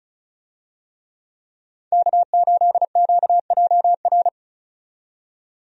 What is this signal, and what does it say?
Morse code, heard as a keyed tone, about 36 wpm, 700 Hz. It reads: K8QJR
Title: USA Call Signs - 5 Characters - 35wpm